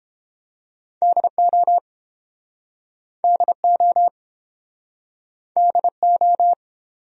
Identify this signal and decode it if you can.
Morse code, heard as a keyed tone, about 34 wpm, 700 Hz. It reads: DO DO DO